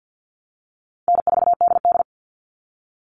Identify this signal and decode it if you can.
Morse code, heard as a keyed tone, about 48 wpm, 700 Hz. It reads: N4DD
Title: Easy Call Signs Repeated 3x - 50wpm